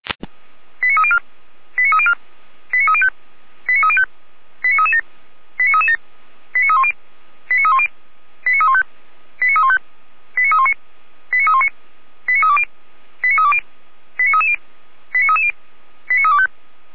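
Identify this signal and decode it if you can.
Five-tone selective call (ZVEI), heard as quick runs of five tones, tones of about 70 ms. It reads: 89304 89304 89305 89305 89337 89337 89310 89310 89315 89315 89319 89319 89320 89320 89330 89330 89325
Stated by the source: ZVEI five-tone call received by SDR, recorded by fire brigade Feuerwehr Orlamünde.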